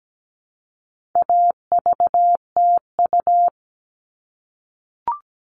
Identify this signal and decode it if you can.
Morse code, heard as a keyed tone, about 17 wpm, 700 Hz. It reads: AVTU
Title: All Cut Numbers - 4-Digit 17wpm